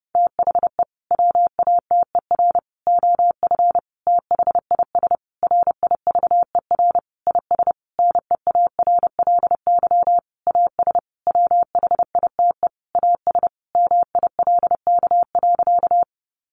Morse code, heard as a keyed tone, about 30 wpm, 700 Hz. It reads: THE WATER OF THIS RIVER IS NEARLY AS WHITE AS MILK.
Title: Sentences from Top 1500 Words 30wpm